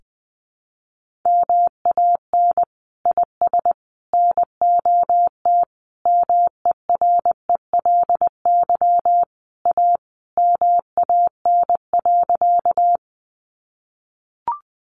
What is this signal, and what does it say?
Morse code, heard as a keyed tone, about 20 wpm, 700 Hz. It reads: MAN IS NOT MERELY A MAN.
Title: Sentences from Top 1400 Words 20wpm